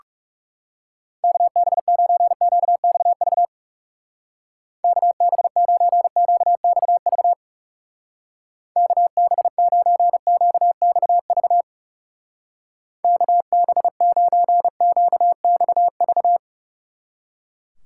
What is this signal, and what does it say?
Morse code, heard as a keyed tone, about 40 wpm, 700 Hz. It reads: KB9QXV KB9QXV KB9QXV KB9QXV